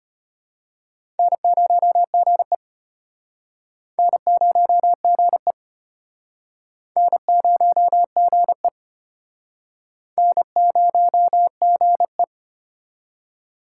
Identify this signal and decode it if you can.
Morse code, heard as a keyed tone, about 33 wpm, 700 Hz. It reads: N0GE N0GE N0GE N0GE